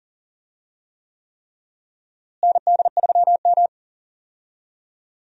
Morse code, heard as a keyed tone, about 40 wpm, 700 Hz. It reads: ND3M
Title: USA Call Signs - 4 Characters - Repeated 3x - 40wpm